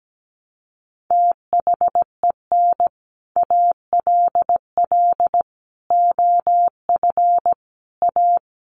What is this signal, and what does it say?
Morse code, heard as a keyed tone, about 17 wpm, 700 Hz. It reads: THEN ALL OF A
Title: Sets of 5 Words Encore 17wpm